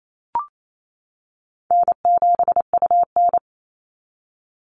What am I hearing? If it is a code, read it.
Morse code, heard as a keyed tone, about 28 wpm, 700 Hz. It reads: N7UN
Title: NA SOTA Chaser Callsigns 28wpm